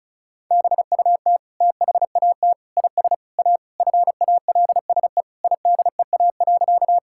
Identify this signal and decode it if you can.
Morse code, heard as a keyed tone, about 34 wpm, 700 Hz. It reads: BUT THAT IS A FALSE IDEA.
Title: Sentences from Top 1600 Words 35wpm